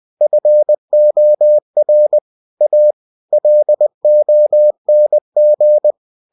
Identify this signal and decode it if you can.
Morse code, heard as a keyed tone, about 20 wpm, 600 Hz. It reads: FOR A LONG